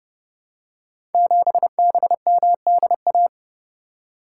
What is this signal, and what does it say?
Morse code, heard as a keyed tone, about 30 wpm, 700 Hz. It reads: 7BMDA
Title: Code Groups - 5 Letter:Numbers - 30wpm